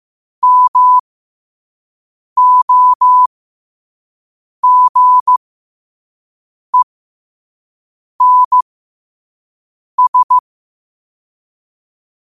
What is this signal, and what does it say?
Morse code, heard as a keyed tone, about 15 wpm, spaced as for 5 wpm, 1000 Hz. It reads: MOGENS
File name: qotdfiles / 5wpm